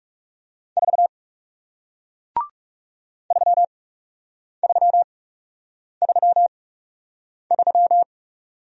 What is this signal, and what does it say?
Morse code, heard as a keyed tone, about 39 wpm, spaced as for 8 wpm, 700 Hz. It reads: 4 3333